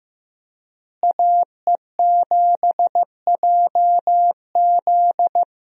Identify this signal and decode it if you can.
Morse code, heard as a keyed tone, about 15 wpm, 700 Hz. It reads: AE7JZ
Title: USA Call Signs - 5 Characters - Repeated 3x - 15wpm